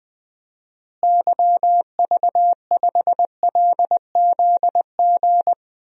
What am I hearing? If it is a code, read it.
Morse code, heard as a keyed tone, about 20 wpm, 700 Hz. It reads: YV5LZG
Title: DXCC-100 Call Signs - 6 Characters - Repeated 3x - 20wpm